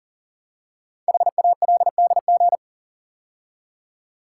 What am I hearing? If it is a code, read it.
Morse code, heard as a keyed tone, about 42 wpm, 700 Hz. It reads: HALDG
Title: Code Groups - 5 Letters - 40wpm